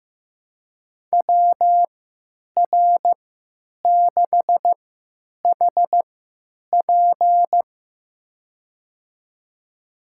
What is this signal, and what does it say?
Morse code, heard as a keyed tone, about 15 wpm, spaced as for 9 wpm, 700 Hz. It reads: WR6HP
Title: USA Call Signs - 5 Characters - 3x Character Spacing - 15wpm